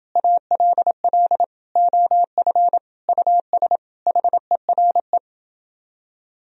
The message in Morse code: ALL OF US HERE

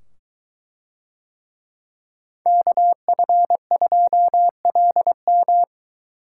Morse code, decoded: KF2LM

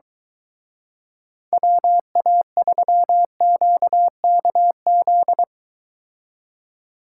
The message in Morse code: WA3QKZ

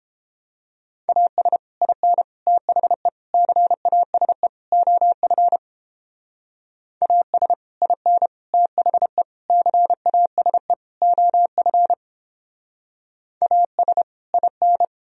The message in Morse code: AS IN THE CASE OF AS IN THE CASE OF AS IN

